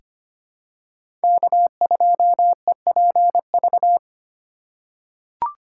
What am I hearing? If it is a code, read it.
Morse code, heard as a keyed tone, about 25 wpm, 700 Hz. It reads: K2EPV
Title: USA Call Signs - 5 Characters - 4x Character Spacing - 25wpm